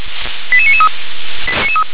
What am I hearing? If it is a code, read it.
Five-tone selective call (ZVEI), heard as quick runs of five tones, tones of about 70 ms. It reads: 88993
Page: SDR/ZVEI